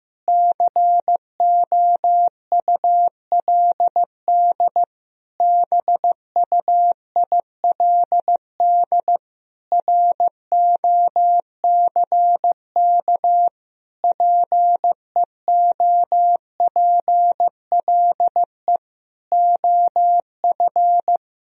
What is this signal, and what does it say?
Morse code, heard as a keyed tone, about 15 wpm, 700 Hz. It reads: COULD BUILD ROCK PEOPLE OF
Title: Sentences from Top 1000 Words - Review Part 2 - 15wpm